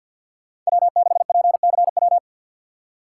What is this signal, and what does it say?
Morse code, heard as a keyed tone, about 51 wpm, 690 Hz. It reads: W6PCW
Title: USA Call Signs - 3x Character Spacing - 50wpm